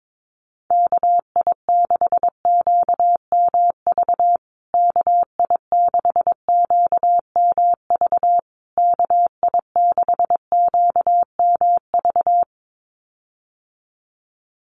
Morse code, heard as a keyed tone, about 22 wpm, 700 Hz. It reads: KI6QMV KI6QMV KI6QMV
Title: USA Call Signs Repeated 3x 22wpm